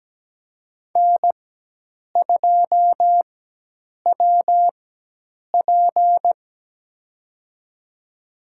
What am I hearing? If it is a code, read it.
Morse code, heard as a keyed tone, about 17 wpm, spaced as for 8 wpm, 700 Hz. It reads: N2WP
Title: USA Call Signs - 4 Characters - 4x Character Spacing - 17wpm